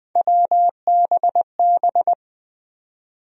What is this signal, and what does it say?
Morse code, heard as a keyed tone, about 20 wpm, 700 Hz. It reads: WBB
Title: USA Call Sign Postfixes - 3 Letters 20wpm